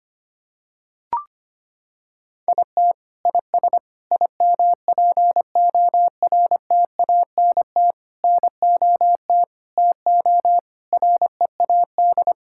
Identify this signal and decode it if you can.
Morse code, heard as a keyed tone, about 25 wpm, 700 Hz. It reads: IT IS IMPORTANT NOT TO READ
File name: Sets of 9 Words - 25wpm